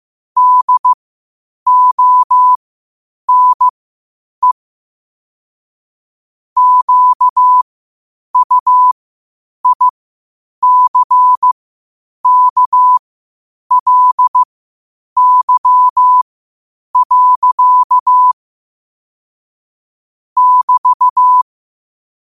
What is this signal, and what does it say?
Morse code, heard as a keyed tone, about 15 wpm, spaced as for 8 wpm, 1000 Hz. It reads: DONE QUICKLY. <BT>
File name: qotdfiles / 8wpm